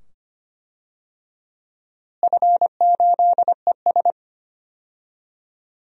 Morse code, heard as a keyed tone, about 25 wpm, 700 Hz. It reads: F8ES
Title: DXCC-100 Call Signs - 4 Characters - Repeated 2x - 25wpm